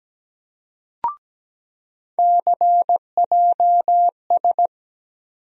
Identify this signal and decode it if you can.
Morse code, heard as a keyed tone, about 17 wpm, 700 Hz. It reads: CJS
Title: USA Call Sign Postfixes - 3 Letters 17wpm